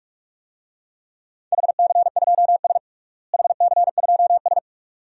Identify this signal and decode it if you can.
Morse code, heard as a keyed tone, about 46 wpm, 700 Hz. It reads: HC2S HC2S
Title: DXCC-100 Call Signs - 4 Characters - Repeated 2x - 45wpm